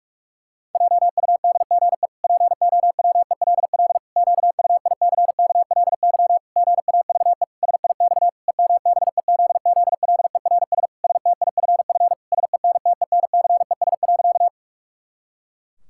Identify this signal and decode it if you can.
Morse code, heard as a keyed tone, about 46 wpm, 700 Hz. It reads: JUDGE POWELL QUICKLY GAVE SIX EMBEZZLERS STIFF SENTENCES.
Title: Pangram Challenge v2 - 45wpm